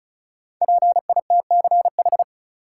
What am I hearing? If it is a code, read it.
Morse code, heard as a keyed tone, about 36 wpm, 700 Hz. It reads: PITCH
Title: Sentences from Top 1300 Words - New Word Drill - 35wpm